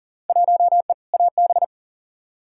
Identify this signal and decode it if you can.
Morse code, heard as a keyed tone, about 40 wpm, 700 Hz. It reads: 1E AB
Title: Field Day Exchanges - 40wpm